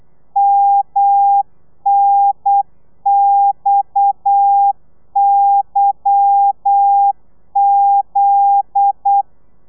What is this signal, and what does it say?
Morse code, heard as a keyed tone, about 8 wpm, 800 Hz. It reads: MNXYZ
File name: telegrafia / i2tfj / Lez10 35